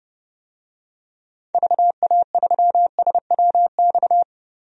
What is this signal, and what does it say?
Morse code, heard as a keyed tone, about 30 wpm, 700 Hz. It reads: VA3SWX
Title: DXCC-100 Call Signs - 6 Characters - 30wpm